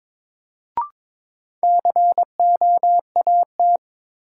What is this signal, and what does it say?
Morse code, heard as a keyed tone, about 22 wpm, 700 Hz. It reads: COAT
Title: Sentences from Top 1600 Words - New Word Drill Alt - 22wpm